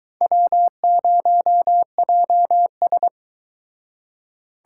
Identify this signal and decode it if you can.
Morse code, heard as a keyed tone, about 23 wpm, 700 Hz. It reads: W0JS